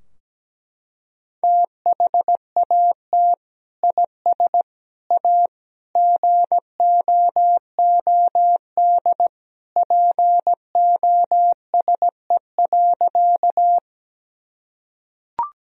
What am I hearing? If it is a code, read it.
Morse code, heard as a keyed tone, about 17 wpm, 700 Hz. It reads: THAT IS A GOOD POSE.
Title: Sentences from Top 1000 Words - Review Part 2 - 17wpm